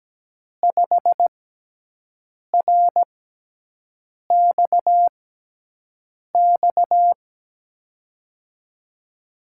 Morse code, heard as a keyed tone, about 17 wpm, spaced as for 6 wpm, 700 Hz. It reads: 5RXX
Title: USA Call Signs - 5 Characters - 6x Character Spacing - 17wpm